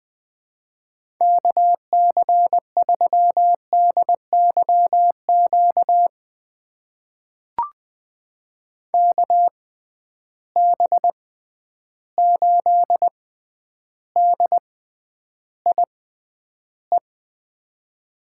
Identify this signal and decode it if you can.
Morse code, heard as a keyed tone, about 20 wpm, spaced as for 10 wpm, 700 Hz. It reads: KC3DYQ KB8DIE
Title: USA Call Signs - 6x Character Spacing - 20wpm